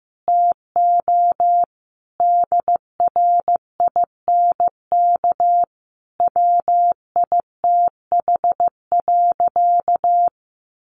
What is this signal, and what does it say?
Morse code, heard as a keyed tone, about 15 wpm, 700 Hz. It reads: TO DRINK WITH.
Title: Sentences from Top 1500 Words 15wpm